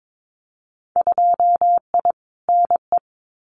Morse code, heard as a keyed tone, about 22 wpm, 700 Hz. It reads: 2I NE
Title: Winter Field Day Exchanges Encore 22wpm